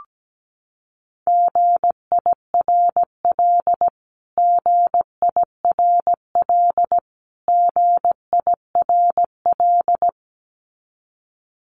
Morse code, heard as a keyed tone, about 17 wpm, 700 Hz. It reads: GIRL GIRL GIRL
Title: Top 500 Words Repeated 17wpm